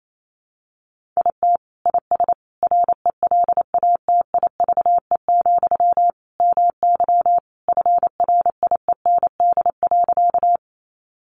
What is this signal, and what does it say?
Morse code, heard as a keyed tone, about 28 wpm, 700 Hz. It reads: IT IS RELATIVE, MY FRIEND.
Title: Sentences from Top 1400 Words 28wpm